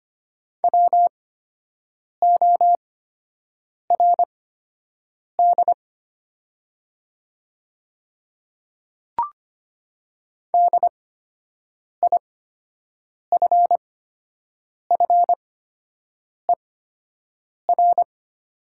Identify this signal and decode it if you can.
Morse code, heard as a keyed tone, about 25 wpm, spaced as for 7 wpm, 700 Hz. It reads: WORD DIFFER
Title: 8x Character Spacing - Top 200 Words - 25wpm